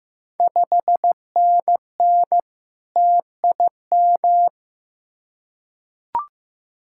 Morse code, heard as a keyed tone, about 15 wpm, 690 Hz. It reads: 5NN TIM
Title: Ontario QSO Party Exchange 15wpm